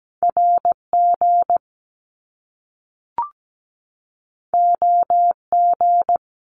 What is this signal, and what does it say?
Morse code, heard as a keyed tone, about 17 wpm, 700 Hz. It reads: RG OG
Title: Top Two Letter Combinations in Words 17wpm